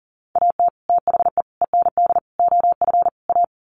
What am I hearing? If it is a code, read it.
Morse code, heard as a keyed tone, about 38 wpm, 700 Hz. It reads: AT THE END OF A
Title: Sets of 5 Words 40wpm